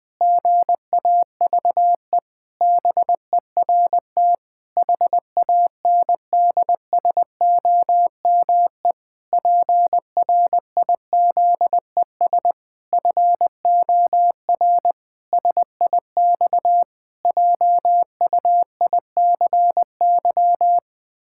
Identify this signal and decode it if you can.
Morse code, heard as a keyed tone, about 21 wpm, 700 Hz. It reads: GAVE BERT HANDSOME PRIZES FOR SIX JUICY
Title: Pangram Challenge v2 - 20wpm